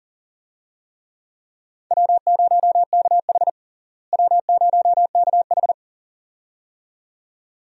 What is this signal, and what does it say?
Morse code, heard as a keyed tone, about 40 wpm, 700 Hz. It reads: W0KH W0KH
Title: USA Call Signs - 4 Characters - Repeated 2x - 40wpm